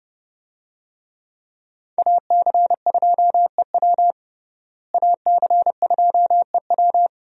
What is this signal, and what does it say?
Morse code, heard as a keyed tone, about 30 wpm, 700 Hz. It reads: AC2EW AC2EW
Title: USA Call Signs - 5 Characters - Repeated 3x - 30wpm